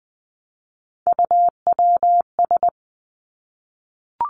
Morse code, heard as a keyed tone, about 19 wpm, 700 Hz. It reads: UWS
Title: Three Challenging Letter-Number Combinations 20wpm